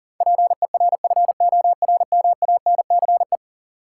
Morse code, heard as a keyed tone, about 37 wpm, 700 Hz. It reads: PERFORMANCE
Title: Sentences from Top 1000 Words - New Word Drill - 40wpm